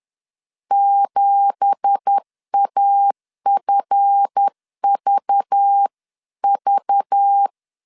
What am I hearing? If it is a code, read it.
Morse code, heard as a keyed tone, about 10 wpm, 800 Hz. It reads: 7AFVV
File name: CW / DET / 1 9 1500C S A 10wpm 4511